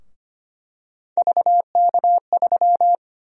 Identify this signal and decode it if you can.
Morse code, heard as a keyed tone, about 25 wpm, 700 Hz. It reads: VK3
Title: USA Call Sign Prefixes - Repeated 2x - 25wpm